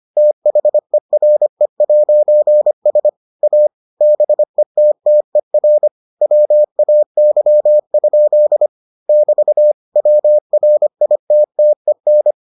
Morse code, heard as a keyed tone, about 25 wpm, 600 Hz. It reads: THERE'S A BETTER WAY? <BT> WRITTEN